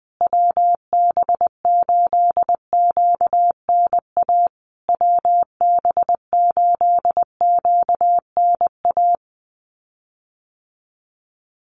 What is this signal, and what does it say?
Morse code, heard as a keyed tone, about 19 wpm, 700 Hz. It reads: WB8QNA WB8QNA
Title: USA Call Signs Repeated 3x 20wpm